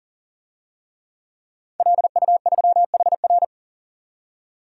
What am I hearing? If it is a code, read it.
Morse code, heard as a keyed tone, about 40 wpm, 700 Hz. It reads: LU3HR
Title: DXCC-100 Call Signs - 5 Characters - 40wpm